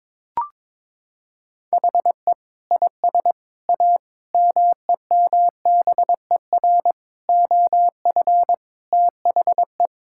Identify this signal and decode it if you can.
Morse code, heard as a keyed tone, about 22 wpm, 700 Hz. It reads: HE IS A MEMBER OF THE
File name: practice / 22wpm / Sentences with 3-Word Phrases - 22wpm